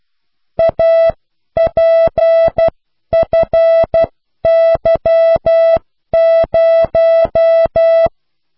Morse code, heard as a keyed tone, about 12 wpm, spaced as for 10 wpm, 660 Hz. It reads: APFY0